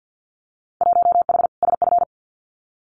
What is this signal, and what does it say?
Morse code, heard as a keyed tone, about 50 wpm, 700 Hz. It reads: 1H SF
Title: Winter Field Day Exchanges 50wpm